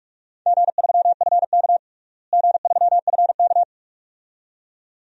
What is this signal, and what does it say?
Morse code, heard as a keyed tone, about 46 wpm, 700 Hz. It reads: G3FK G3FK
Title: DXCC-100 Call Signs - 4 Characters - Repeated 2x - 45wpm